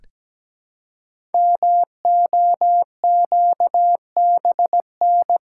Morse code, heard as a keyed tone, about 17 wpm, 700 Hz. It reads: MOQBN
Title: Code Groups - 5 Letters - 17wpm